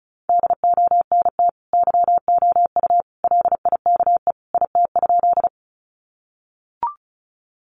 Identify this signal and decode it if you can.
Morse code, heard as a keyed tone, about 34 wpm, 700 Hz. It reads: DONT YOU LIKE IT?
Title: Sentences from Top 300 Words 35wpm